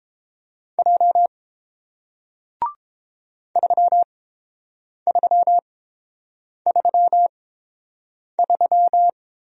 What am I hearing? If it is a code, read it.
Morse code, heard as a keyed tone, about 30 wpm, spaced as for 7 wpm, 700 Hz. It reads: J 3333